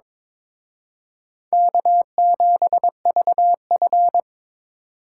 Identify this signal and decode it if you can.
Morse code, heard as a keyed tone, about 26 wpm, 700 Hz. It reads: K7VF
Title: Speed Racing - USA 1x2 Call Signs 22wpm